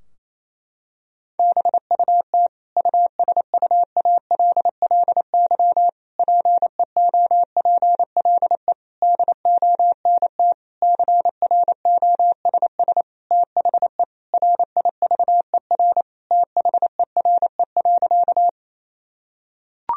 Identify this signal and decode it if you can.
Morse code, heard as a keyed tone, about 28 wpm, 700 Hz. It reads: BUT USUALLY PEOPLE DONT CROSS THE RIVER THERE.
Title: Sentences from Top 1000 Words - Review Part 3 - 28wpm